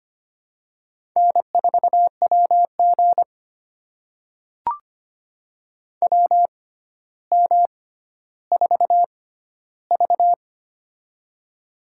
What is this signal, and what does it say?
Morse code, heard as a keyed tone, about 25 wpm, spaced as for 13 wpm, 700 Hz. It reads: N4WG WM4V
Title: USA Call Signs - 4 Characters - 6x Character Spacing - 25wpm